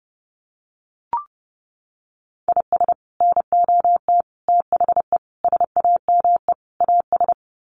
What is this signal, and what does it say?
Morse code, heard as a keyed tone, about 30 wpm, 700 Hz. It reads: IS NOT THE SAME AS
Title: Sets of 5 Words Encore 30wpm